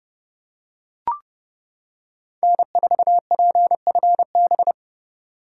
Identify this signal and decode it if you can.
Morse code, heard as a keyed tone, about 28 wpm, 700 Hz. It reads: N4PFB